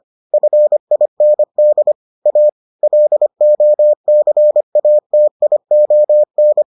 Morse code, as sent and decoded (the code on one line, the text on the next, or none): ..-. .. -. -.. / .- / .-.. --- -.-. .- - .. --- -.
FIND A LOCATION